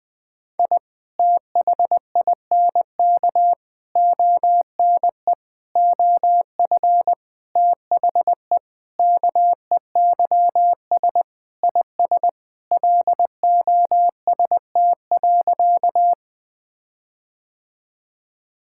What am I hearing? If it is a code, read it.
Morse code, heard as a keyed tone, about 20 wpm, 700 Hz. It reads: I THINK ONE OF THE KEYS IS LOST.
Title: Sentences with 5-Word Phrases - 20wpm